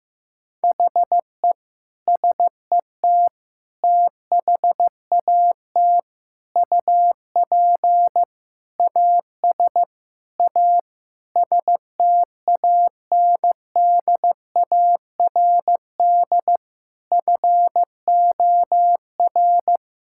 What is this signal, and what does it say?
Morse code, heard as a keyed tone, about 15 wpm, 700 Hz. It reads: HE SET THAT UP AS A STANDARD FOR
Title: Sentences from Top 1000 Words - Review Part 6 - 15wpm